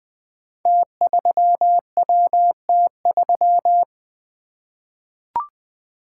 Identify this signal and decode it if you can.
Morse code, heard as a keyed tone, about 21 wpm, 700 Hz. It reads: T3WT3
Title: Code Groups - 5 Letter:Numbers - 20wpm